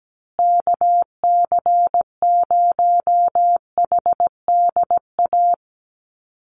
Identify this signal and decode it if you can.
Morse code, heard as a keyed tone, about 17 wpm, 700 Hz. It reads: KC0HDA
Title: USA Call Signs Repeated 2x 17wpm